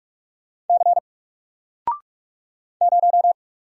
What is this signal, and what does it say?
Morse code, heard as a keyed tone, about 44 wpm, 700 Hz. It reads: C 0
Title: Single Character (Letter:Number:Punctuation:Prosign) 45wpm